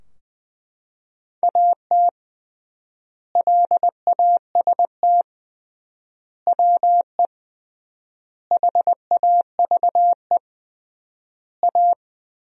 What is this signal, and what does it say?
Morse code, heard as a keyed tone, about 19 wpm, 700 Hz. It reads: AT LAST WE HAVE A
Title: 3x Word-Spacing - Sentences from Top 300 Words - 20wpm